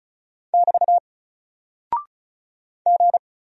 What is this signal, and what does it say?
Morse code, heard as a keyed tone, about 35 wpm, 700 Hz. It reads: <BT> G
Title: Single Character (Letter:Number:Punctuation:Prosign) 35wpm